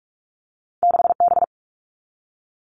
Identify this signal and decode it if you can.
Morse code, heard as a keyed tone, about 46 wpm, 700 Hz. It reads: 6B
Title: All Cut Numbers - 2-Digit 45wpm